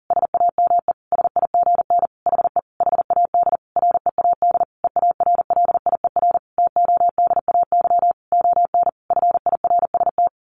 Morse code, heard as a keyed tone, about 40 wpm, 700 Hz. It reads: SAME SIGN HE HAD READ EARLIER TODAY ON FIRST
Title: A Sound Of Thunder 40wpm